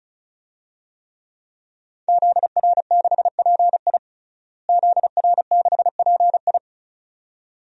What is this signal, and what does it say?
Morse code, heard as a keyed tone, about 36 wpm, 700 Hz. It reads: ZR6PI ZR6PI
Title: DXCC-100 Call Signs - 5 Characters - Repeated 2x - 35wpm